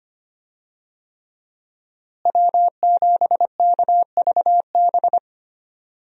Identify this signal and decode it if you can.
Morse code, heard as a keyed tone, about 25 wpm, 700 Hz. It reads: W7KVB
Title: USA Call Signs - 5 Characters - Repeated 3x - 25wpm